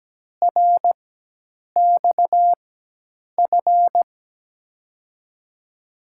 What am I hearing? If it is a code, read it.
Morse code, heard as a keyed tone, about 17 wpm, spaced as for 8 wpm, 700 Hz. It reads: RXF
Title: USA Call Signs - 4x Character Spacing - 17wpm